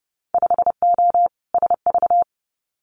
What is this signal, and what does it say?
Morse code, heard as a keyed tone, about 31 wpm, 700 Hz. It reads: 5O SV
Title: Winter Field Day Exchanges Encore 30wpm